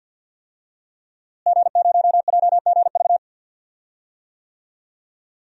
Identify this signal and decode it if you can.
Morse code, heard as a keyed tone, about 50 wpm, 700 Hz. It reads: G0JGV
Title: DXCC-100 Call Signs - 5 Characters - Repeated 3x - 50wpm